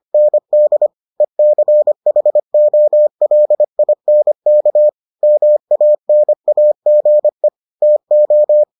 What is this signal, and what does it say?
Morse code, heard as a keyed tone, about 25 wpm, 600 Hz. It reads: ND ECHOLINK MANAGE TO